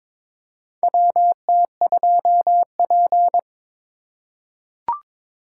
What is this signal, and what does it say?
Morse code, heard as a keyed tone, about 22 wpm, 700 Hz. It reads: WT2P
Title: USA Call Signs - 4 Characters - 2x Character Spacing - 22wpm